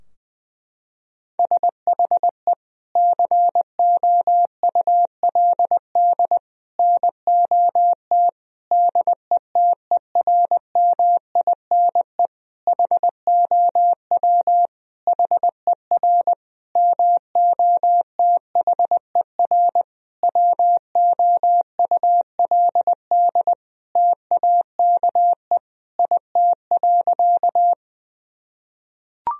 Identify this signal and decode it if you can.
Morse code, heard as a keyed tone, about 21 wpm, 700 Hz. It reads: SHE COULD NOT DETERMINE HOW HER MOTHER WOULD TAKE IT.
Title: Sentences from Top 1000 Words - Review Part 4 - 20wpm